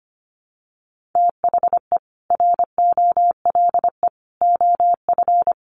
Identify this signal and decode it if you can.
Morse code, heard as a keyed tone, about 25 wpm, 700 Hz. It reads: THE ROLE OF
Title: Sets of 3 Words 25wpm